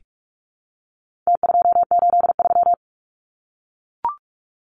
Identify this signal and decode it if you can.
Morse code, heard as a keyed tone, about 44 wpm, 700 Hz. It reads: T283
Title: Numbers - Contest Style - 4-Digit 45wpm